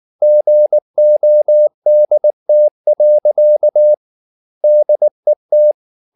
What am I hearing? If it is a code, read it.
Morse code, heard as a keyed tone, about 19 wpm, 600 Hz. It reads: GODT. DET
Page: Ferie 19 wpm